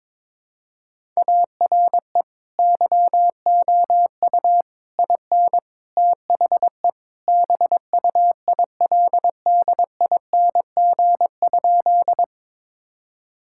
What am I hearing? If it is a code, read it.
Morse code, heard as a keyed tone, about 22 wpm, 700 Hz. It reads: ARE YOU IN THE BUILDING?
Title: Sentences from Top 1000 Words - Review Part 5 - 22wpm